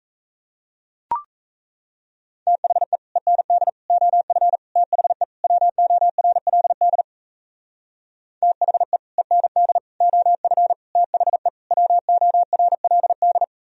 Morse code, heard as a keyed tone, about 37 wpm, 700 Hz. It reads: THE END OF THE WORLD THE END OF THE WORLD